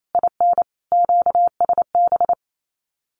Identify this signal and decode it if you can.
Morse code, heard as a keyed tone, about 28 wpm, 700 Hz. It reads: IN QSB
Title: Common QSO Phrases 28wpm